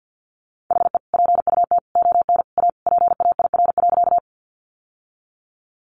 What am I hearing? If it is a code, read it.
Morse code, heard as a keyed tone, about 48 wpm, 700 Hz. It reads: HE PUT ON A PAIR.
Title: Sentences from Top 1200 Words 50wpm